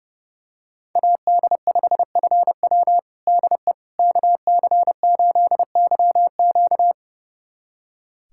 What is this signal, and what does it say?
Morse code, heard as a keyed tone, about 30 wpm, 700 Hz. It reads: AD5FW DE KC8YQ
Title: USA Callsign DE Callsign - 5 Characters - 30wpm